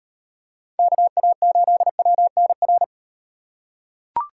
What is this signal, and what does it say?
Morse code, heard as a keyed tone, about 36 wpm, 700 Hz. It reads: KA8WNR